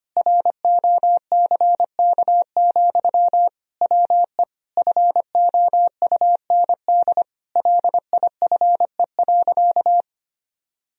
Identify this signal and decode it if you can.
Morse code, heard as a keyed tone, about 25 wpm, 700 Hz. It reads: ROCK, WE FOUND LIFE.
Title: Sentences from Top 1000 Words - Review Part 2 - 25wpm